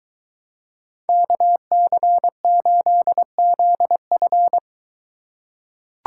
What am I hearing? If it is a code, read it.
Morse code, heard as a keyed tone, about 23 wpm, 700 Hz. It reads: KC8ZF